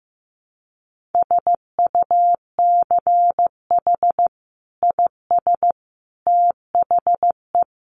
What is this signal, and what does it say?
Morse code, heard as a keyed tone, about 15 wpm, 700 Hz. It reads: SUCH IS THE
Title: Sentences from Top 1400 Words 15wpm